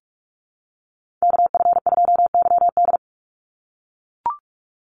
Morse code, heard as a keyed tone, about 44 wpm, 700 Hz. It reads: KF2YD
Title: USA Call Signs 45wpm